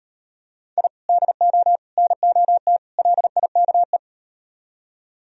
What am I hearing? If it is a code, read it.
Morse code, heard as a keyed tone, about 38 wpm, 700 Hz. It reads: I DO NOT LIKE